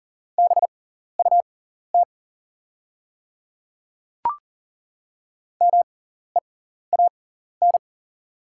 Morse code, heard as a keyed tone, about 38 wpm, spaced as for 14 wpm, 700 Hz. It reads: BUT MEAN
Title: 6x Character Spacing - Top 300 Words - 40wpm